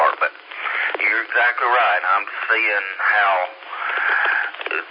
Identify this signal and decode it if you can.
Emergency Alert System (SAME) tones, heard as harsh buzzing bursts, with speech. It is heard from 0.7 to 4.6 s.